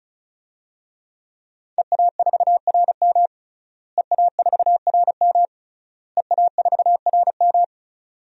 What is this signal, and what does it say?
Morse code, heard as a keyed tone, about 34 wpm, 700 Hz. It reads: EA4RM EA4RM EA4RM
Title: DXCC-100 Call Signs - 5 Characters - Repeated 3x - 35wpm